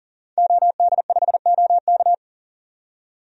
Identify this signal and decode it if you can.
Morse code, heard as a keyed tone, about 39 wpm, 700 Hz. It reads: OD5OK
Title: DXCC-100 Call Signs - 5 Characters - 40wpm